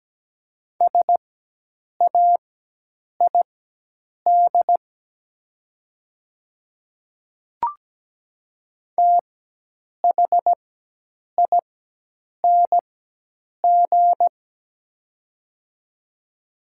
Morse code, heard as a keyed tone, about 17 wpm, spaced as for 8 wpm, 700 Hz. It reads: SAID THING
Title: 4x Character Spacing - Top 100 Words - 17wpm